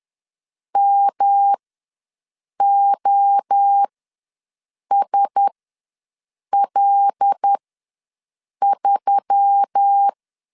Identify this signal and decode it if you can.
Morse code, heard as a keyed tone, about 10 wpm, spaced as for 6 wpm, 800 Hz. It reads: MOSL3
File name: LEZIONI / 5 LBVC / 5 L ws 10wpm 3636